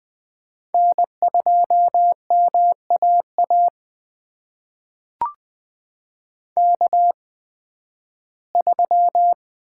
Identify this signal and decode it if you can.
Morse code, heard as a keyed tone, about 20 wpm, spaced as for 13 wpm, 700 Hz. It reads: N2MAA K3
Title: USA Call Signs - 5 Characters - 8x Character Spacing - 20wpm